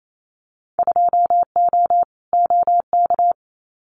Morse code, heard as a keyed tone, about 28 wpm, 700 Hz. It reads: 2O OK
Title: Winter Field Day Exchanges Encore 28wpm